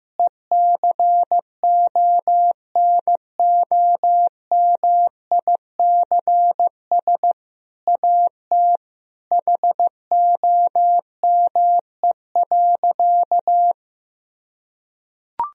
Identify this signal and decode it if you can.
Morse code, heard as a keyed tone, about 15 wpm, 700 Hz. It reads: ECONOMICS AT HOME.
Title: Sentences from Top 1000 Words - Review Part 3 - 15wpm